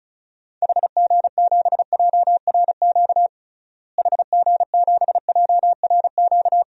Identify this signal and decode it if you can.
Morse code, heard as a keyed tone, about 34 wpm, 700 Hz. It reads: HG7JRQ HG7JRQ
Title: DXCC-100 Call Signs - 6 Characters - Repeated 2x - 35wpm